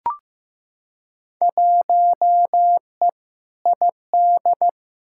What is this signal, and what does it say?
Morse code, heard as a keyed tone, about 15 wpm, 700 Hz. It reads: 1E ID
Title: Field Day Exchanges - 15wpm